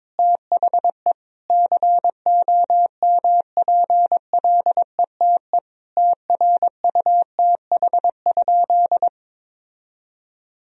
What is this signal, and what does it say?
Morse code, heard as a keyed tone, about 22 wpm, 700 Hz. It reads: THE COMPLETE TRUTH?
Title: Sentences from Top 1000 Words - Review Part 2 - 22wpm